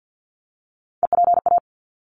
Morse code, heard as a keyed tone, about 50 wpm, 700 Hz. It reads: EPA
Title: ARRL Sections 50wpm